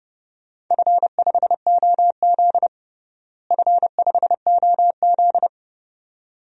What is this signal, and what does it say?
Morse code, heard as a keyed tone, about 30 wpm, 700 Hz. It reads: F5OZ F5OZ
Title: DXCC-100 Call Signs - 4 Characters - Repeated 2x - 30wpm